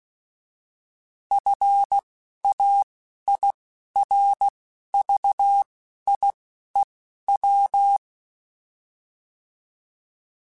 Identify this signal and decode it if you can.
Morse code, heard as a keyed tone, about 16 wpm, spaced as for 11 wpm, 800 Hz. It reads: FAIRVIEW